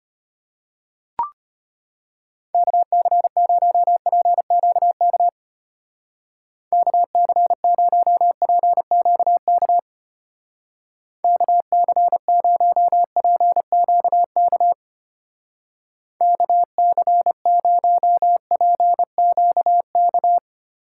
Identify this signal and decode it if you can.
Morse code, heard as a keyed tone, about 32 wpm, 700 Hz. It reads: KC0PQK KC0PQK KC0PQK KC0PQK